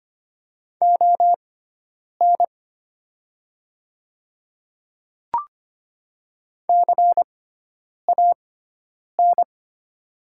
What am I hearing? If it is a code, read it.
Morse code, heard as a keyed tone, about 25 wpm, spaced as for 9 wpm, 700 Hz. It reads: ON CAN
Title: 6x Character Spacing - Top 100 Words - 25wpm